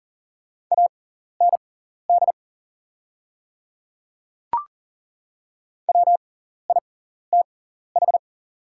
Morse code, heard as a keyed tone, about 39 wpm, spaced as for 14 wpm, 690 Hz. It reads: AND WITH
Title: 6x Character Spacing - Top 100 Words - 40wpm